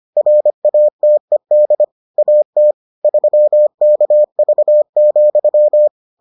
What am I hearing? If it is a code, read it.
Morse code, heard as a keyed tone, about 25 wpm, 600 Hz. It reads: RATED AT 3KV,